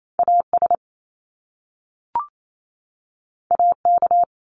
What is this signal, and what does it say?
Morse code, heard as a keyed tone, about 28 wpm, 700 Hz. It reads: AS AK
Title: Top Two Letter Combinations in Words 28wpm